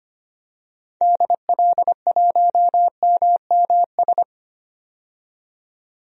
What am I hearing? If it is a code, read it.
Morse code, heard as a keyed tone, about 25 wpm, 700 Hz. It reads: DL1MMS
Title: DXCC-100 Call Signs - 6 Characters - 25wpm